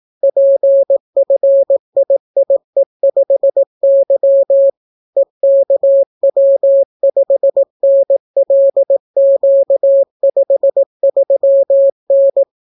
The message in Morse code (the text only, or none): PFIIE5Y EKW5NLQ53N